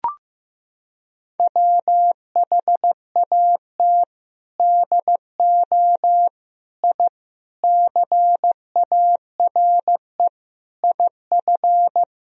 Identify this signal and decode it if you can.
Morse code, heard as a keyed tone, about 15 wpm, 700 Hz. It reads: WHAT DO I CARE IF
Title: Sentences from Top 1000 Words - Review Part 1 - 15wpm